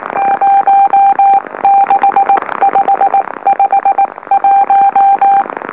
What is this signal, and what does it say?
Morse code, heard as a keyed tone, about 19 wpm, spaced as for 16 wpm, 790 Hz. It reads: 06551
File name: M04- -2345utc-LOLO